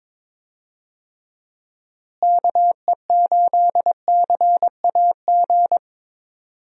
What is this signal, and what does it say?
Morse code, heard as a keyed tone, about 23 wpm, 700 Hz. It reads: KE8CAG
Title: USA Call Signs - 6 Characters - Repeated 2x - 22wpm